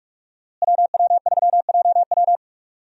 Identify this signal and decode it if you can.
Morse code, heard as a keyed tone, about 45 wpm, 700 Hz. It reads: WW3JW
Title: USA Call Signs - 5 Characters - Repeated 3x - 45wpm